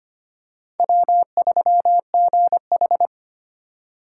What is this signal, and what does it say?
Morse code, heard as a keyed tone, about 25 wpm, 700 Hz. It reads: W3GH
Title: USA Call Signs - 2x Character Spacing - 25wpm